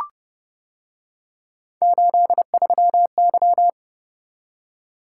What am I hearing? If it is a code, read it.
Morse code, heard as a keyed tone, about 28 wpm, 700 Hz. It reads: 83Y